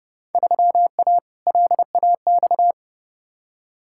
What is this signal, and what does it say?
Morse code, heard as a keyed tone, about 30 wpm, 700 Hz. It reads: 3A LAX
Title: Field Day Exchanges - 30wpm